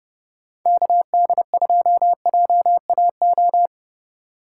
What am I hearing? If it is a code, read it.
Morse code, heard as a keyed tone, about 29 wpm, 700 Hz. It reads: KD2JAO